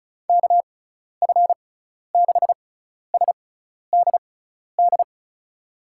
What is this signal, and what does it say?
Morse code, heard as a keyed tone, about 36 wpm, spaced as for 12 wpm, 700 Hz. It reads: KF6SDD
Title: USA Call Signs - 6 Characters - 6x Character Spacing - 35wpm